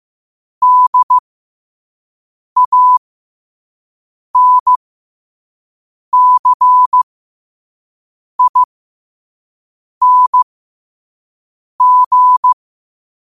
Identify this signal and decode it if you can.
Morse code, heard as a keyed tone, about 15 wpm, spaced as for 5 wpm, 1000 Hz. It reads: DANCING
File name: qotdfiles / 5wpm